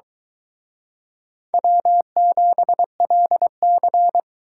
Morse code, heard as a keyed tone, about 24 wpm, 700 Hz. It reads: W7LC